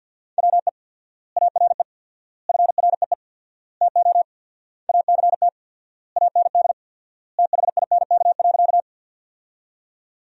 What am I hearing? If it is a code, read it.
Morse code, heard as a keyed tone, about 52 wpm, 700 Hz. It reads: WE ARE FREE TO ACT AND THINK.
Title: 4x Word-Spacing - Sentences from Top 500 Words - 50wpm